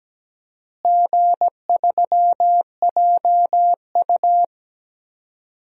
Morse code, heard as a keyed tone, about 17 wpm, 700 Hz. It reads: G3JU
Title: DXCC-100 Call Signs - 4 Characters - Repeated 2x - 17wpm